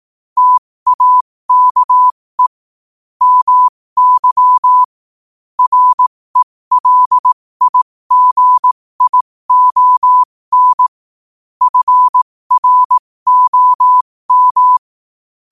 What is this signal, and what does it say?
Morse code, heard as a keyed tone, about 18 wpm, spaced as for 15 wpm, 1000 Hz. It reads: TAKE MY RELIGION FROM
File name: qotdfiles / 15wpm